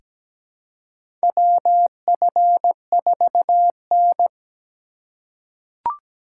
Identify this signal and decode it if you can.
Morse code, heard as a keyed tone, about 17 wpm, 700 Hz. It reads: WF4N
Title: USA Call Signs - 4 Characters - 2x Character Spacing - 17wpm